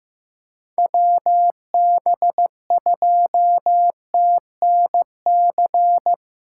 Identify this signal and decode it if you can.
Morse code, heard as a keyed tone, about 15 wpm, 700 Hz. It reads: WB2TNC
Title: USA Call Signs - 6 Characters - 8x Character Spacing - 15wpm